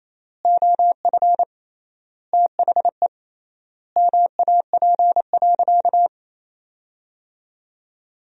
Morse code, heard as a keyed tone, about 28 wpm, 700 Hz. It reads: OF THE MAP.
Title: 3x Word-Spacing - Sentences from Top 400 Words - 28wpm